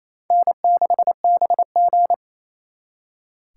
Morse code, heard as a keyed tone, about 36 wpm, 700 Hz. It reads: N6BG